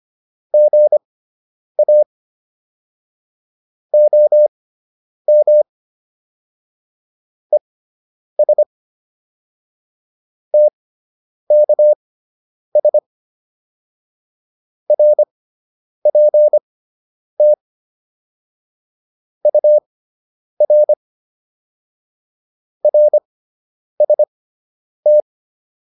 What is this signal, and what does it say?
Morse code, heard as a keyed tone, about 25 wpm, spaced as for 9 wpm, 600 Hz. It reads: GA OM ES TKS RPT UR RST
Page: fundamental-practice-files